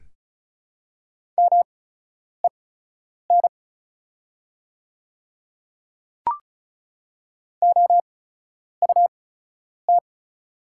Morse code, heard as a keyed tone, about 36 wpm, spaced as for 10 wpm, 700 Hz. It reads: MEN OUT
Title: 8x Character Spacing - Top 200 Words - 35wpm